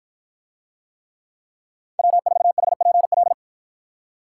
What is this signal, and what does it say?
Morse code, heard as a keyed tone, about 53 wpm, 700 Hz. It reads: W4HPL